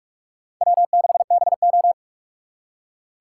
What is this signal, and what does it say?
Morse code, heard as a keyed tone, about 56 wpm, 700 Hz. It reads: W6BO